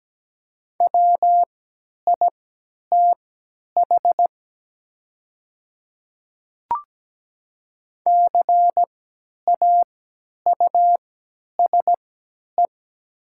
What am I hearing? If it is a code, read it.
Morse code, heard as a keyed tone, about 17 wpm, spaced as for 10 wpm, 700 Hz. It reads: WITH CAUSE
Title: 3x Character Spacing - Top 200 Words - 17wpm